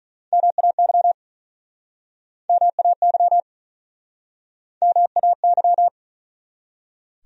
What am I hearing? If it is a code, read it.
Morse code, heard as a keyed tone, about 46 wpm, 700 Hz. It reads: MAY MAY MAY